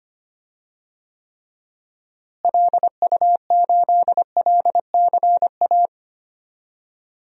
Morse code, heard as a keyed tone, about 25 wpm, 700 Hz. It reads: LU8LCA
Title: DXCC-100 Call Signs - 6 Characters - Repeated 2x - 25wpm